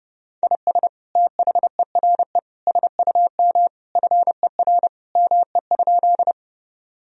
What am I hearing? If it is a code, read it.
Morse code, heard as a keyed tone, about 30 wpm, 700 Hz. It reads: IS THERE SUM FER ME?
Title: Sentences from Top 200 Words and Abbreviations 30wpm